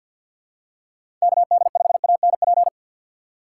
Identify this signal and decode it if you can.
Morse code, heard as a keyed tone, about 52 wpm, 700 Hz. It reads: KD5ANP
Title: USA Call Signs - 3x Character Spacing - 50wpm